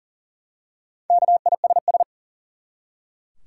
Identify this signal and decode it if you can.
Morse code, heard as a keyed tone, about 40 wpm, 700 Hz. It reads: KISS
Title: Four Letter Words - 40wpm